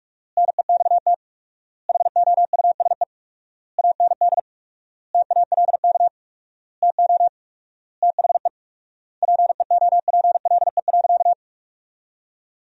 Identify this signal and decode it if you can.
Morse code, heard as a keyed tone, about 46 wpm, 700 Hz. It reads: NEXT HOUSE AND TALK TO THE PEOPLE.
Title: 4x Word-Spacing - Sentences from Top 400 Words - 45wpm